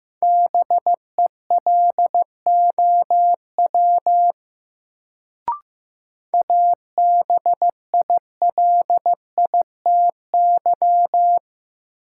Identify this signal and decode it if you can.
Morse code, heard as a keyed tone, about 15 wpm, 700 Hz. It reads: BELOW ABILITY
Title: Sentences from Top 1200 Words - New Word Drill Alt - 15wpm